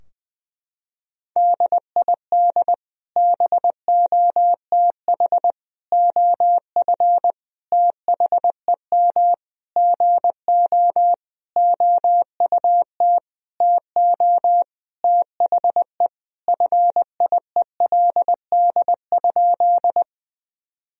Morse code, heard as a keyed tone, about 20 wpm, 700 Hz. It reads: DID BOTH OF THEM GO OUT TO THE FIELD?
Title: Sentences from Top 1000 Words - Review Part 2 - 20wpm